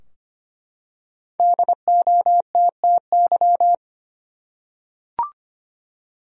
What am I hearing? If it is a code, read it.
Morse code, heard as a keyed tone, about 25 wpm, 700 Hz. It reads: DOTTY
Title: USA Common Names - Max 5 Letters - 25wpm